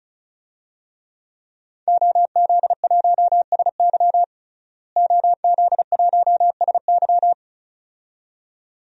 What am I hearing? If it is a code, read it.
Morse code, heard as a keyed tone, about 36 wpm, 700 Hz. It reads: OZ1SY OZ1SY
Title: DXCC-100 Call Signs - 5 Characters - Repeated 2x - 35wpm